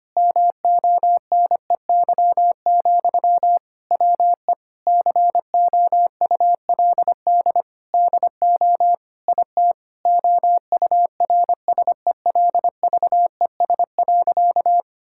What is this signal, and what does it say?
Morse code, heard as a keyed tone, about 25 wpm, 700 Hz. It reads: MONEY, WE COULD DO IT OURSELVES.
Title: Sentences from Top 1600 Words 25wpm